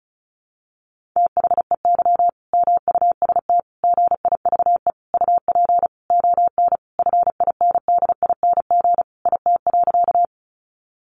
Morse code, heard as a keyed tone, about 34 wpm, 700 Hz. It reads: THEY MUST GIVE UP ON FINDING IT.
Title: Sentences from Top 1100 Words 35wpm